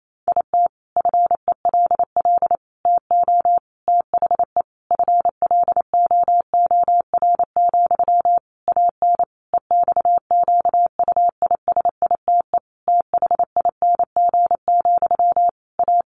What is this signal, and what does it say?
Morse code, heard as a keyed tone, about 28 wpm, 700 Hz. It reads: IT FELL TO THE FLOOR, AN EXQUISITE THING, A
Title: A Sound Of Thunder 28wpm